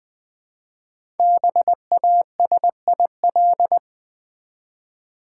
Morse code, heard as a keyed tone, about 20 wpm, 700 Hz. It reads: BASIL
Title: Easy Words with Hints - 20wpm